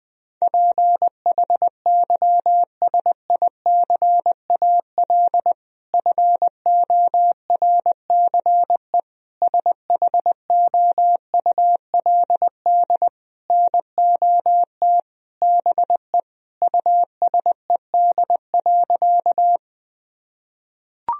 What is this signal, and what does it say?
Morse code, heard as a keyed tone, about 21 wpm, 700 Hz. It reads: PHYSICAL FORCE SHOULD NOT BE USED.
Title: Sentences from Top 1000 Words - Review Part 4 - 20wpm